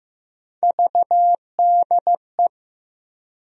Morse code, heard as a keyed tone, about 15 wpm, 700 Hz. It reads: VDE
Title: USA Call Sign Postfixes - 3 Letters 15wpm